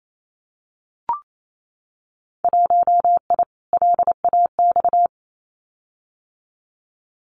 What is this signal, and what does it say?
Morse code, heard as a keyed tone, about 28 wpm, 700 Hz. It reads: 1I LAX
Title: Winter Field Day Exchanges Encore 28wpm